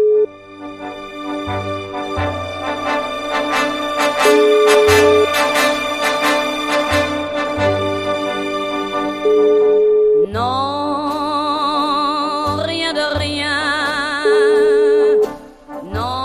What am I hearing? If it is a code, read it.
Ringback tone (Europe, 425 Hz), with noise.